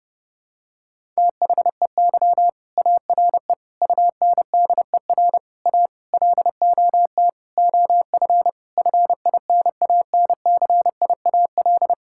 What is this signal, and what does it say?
Morse code, heard as a keyed tone, about 30 wpm, 700 Hz. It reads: THEY ARE UNDER A LOT OF FINANCIAL
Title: Sentences from Top 1000 Words - Review Part 6 - 30wpm